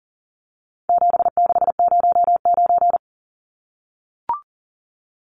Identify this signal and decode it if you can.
Morse code, heard as a keyed tone, about 40 wpm, 700 Hz. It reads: 7609
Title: Numbers - 4-Digit 40wpm